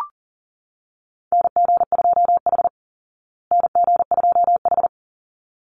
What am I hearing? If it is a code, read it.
Morse code, heard as a keyed tone, about 37 wpm, 700 Hz. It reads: NG2H NG2H
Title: Call Signs Repeated 3x - 40wpm